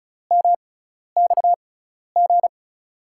Morse code, heard as a keyed tone, about 34 wpm, spaced as for 12 wpm, 700 Hz. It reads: MXG
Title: USA Call Signs - 5 Characters - 6x Character Spacing - 35wpm